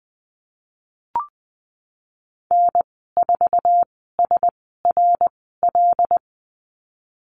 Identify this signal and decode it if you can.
Morse code, heard as a keyed tone, about 20 wpm, spaced as for 15 wpm, 700 Hz. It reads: N4SRL
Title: 088 Lesson 15g - Number 5 - Call Signs 20wpm